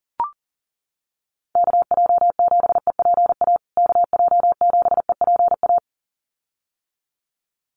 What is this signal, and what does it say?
Morse code, heard as a keyed tone, about 37 wpm, 700 Hz. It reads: KJ7EPA KJ7EPA
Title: USA Call Signs Repeated 2x 40wpm